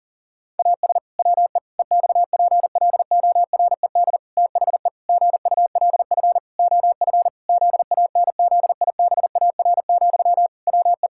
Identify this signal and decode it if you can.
Morse code, heard as a keyed tone, about 40 wpm, 700 Hz. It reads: AS WE EXPLORED THE GULF OF ZANZIBAR, WE
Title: Pangram Challenge v2 - 40wpm